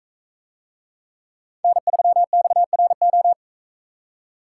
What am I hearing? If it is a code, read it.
Morse code, heard as a keyed tone, about 40 wpm, 700 Hz. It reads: N3XRO